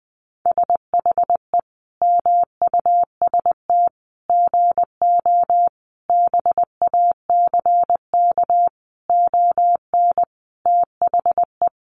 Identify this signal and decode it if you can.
Morse code, heard as a keyed tone, about 20 wpm, 700 Hz. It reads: SHE MUST GO BACK ON THE
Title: Sentences from Top 1200 Words 20wpm